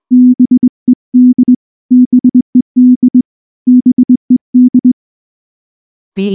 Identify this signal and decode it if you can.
Morse code, heard as a keyed tone, about 20 wpm, 250 Hz. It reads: BED BED BED
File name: morse-training-20WPM